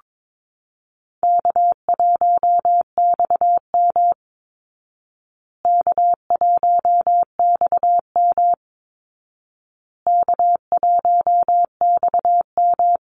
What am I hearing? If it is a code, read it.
Morse code, heard as a keyed tone, about 22 wpm, 700 Hz. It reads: K1XM K1XM K1XM
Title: Easy Call Signs Repeated 3x - 22wpm